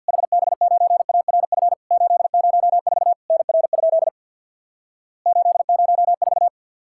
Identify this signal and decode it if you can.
Morse code, heard as a keyed tone, about 50 wpm, 700 Hz. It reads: HB9ARF 804 NR? 804
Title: Worked All Europe DX Contest - QTC Exchange with Repeats - 50wpm